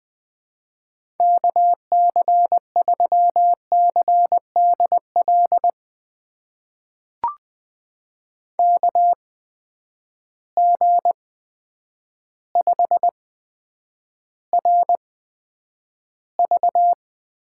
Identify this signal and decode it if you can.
Morse code, heard as a keyed tone, about 20 wpm, spaced as for 9 wpm, 700 Hz. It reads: KC3CDL KG5RV
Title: USA Call Signs - 6 Characters - 8x Character Spacing - 20wpm